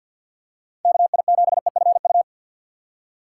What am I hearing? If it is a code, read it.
Morse code, heard as a keyed tone, about 52 wpm, 690 Hz. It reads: KI7EFU
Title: USA Call Signs - 6x Character Spacing - 50wpm